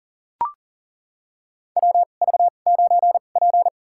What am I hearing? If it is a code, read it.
Morse code, heard as a keyed tone, about 40 wpm, spaced as for 29 wpm, 700 Hz. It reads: WV9P
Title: USA Call Signs - 4 Characters - 2x Character Spacing - 40wpm